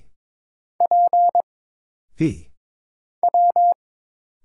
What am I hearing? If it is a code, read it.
Morse code, heard as a keyed tone, about 22 wpm, 700 Hz. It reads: P W
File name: Single Character (Letter:Number:Punctuation:Prosign) - Mind-Melt - 22wpm